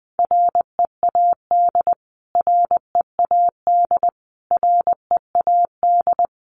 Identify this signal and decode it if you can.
Morse code, heard as a keyed tone, about 20 wpm, 700 Hz. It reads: READ READ READ
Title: Top 500 Words Repeated 20wpm